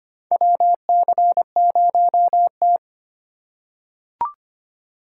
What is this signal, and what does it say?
Morse code, heard as a keyed tone, about 25 wpm, 700 Hz. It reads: WC0T
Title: USA Call Signs - 4 Characters - 4x Character Spacing - 25wpm